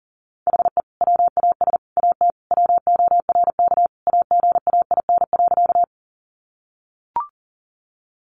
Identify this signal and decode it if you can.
Morse code, heard as a keyed tone, about 40 wpm, 700 Hz. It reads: HE WAS AT WORK AGAIN.
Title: Sentences from Top 200 Words 40wpm